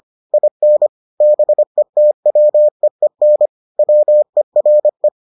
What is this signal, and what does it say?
Morse code, heard as a keyed tone, about 25 wpm, 600 Hz. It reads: IN BETWEEN WERE